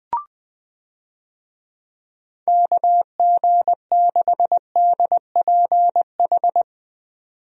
Morse code, heard as a keyed tone, about 20 wpm, 700 Hz. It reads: KG6DPH
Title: USA Call Signs - 6 Characters - 20wpm